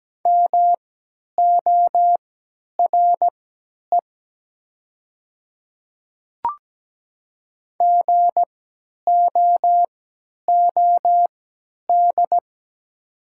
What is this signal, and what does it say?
Morse code, heard as a keyed tone, about 17 wpm, spaced as for 10 wpm, 690 Hz. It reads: MORE GOOD
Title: 3x Character Spacing - Top 200 Words - 17wpm